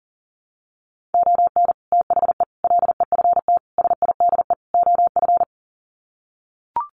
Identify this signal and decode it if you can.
Morse code, heard as a keyed tone, about 40 wpm, 700 Hz. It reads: ON THE LEFT SIDE OF
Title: Sets of 5 Words Encore2 40wpm